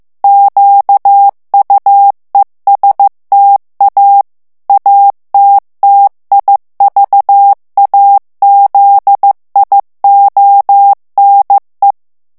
Morse code, heard as a keyed tone, about 15 wpm, 800 Hz. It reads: QUESTA ATTIVAZIONE
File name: ITALIEN texte clair 1 vitesse 14 mots minute